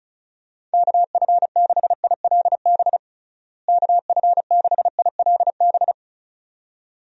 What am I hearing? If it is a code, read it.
Morse code, heard as a keyed tone, about 34 wpm, 700 Hz. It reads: KF6ILB KF6ILB
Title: USA Call Signs - 6 Characters - Repeated 2x - 35wpm